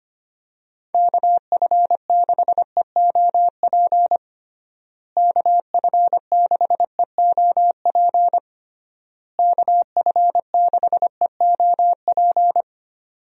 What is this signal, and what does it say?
Morse code, heard as a keyed tone, about 25 wpm, 700 Hz. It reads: KF6EOP KF6EOP KF6EOP
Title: USA Call Signs - 6 Characters - Repeated 3x - 25wpm